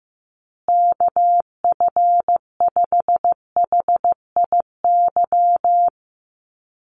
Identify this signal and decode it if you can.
Morse code, heard as a keyed tone, about 15 wpm, 700 Hz. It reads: KF5HIY
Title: USA Call Signs Repeated 3x 15wpm